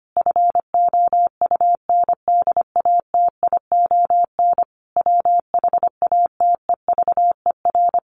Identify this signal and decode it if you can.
Morse code, heard as a keyed tone, about 25 wpm, 700 Hz. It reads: FOUNDATION WHATEVER
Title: Sentences from Top 1200 Words 25wpm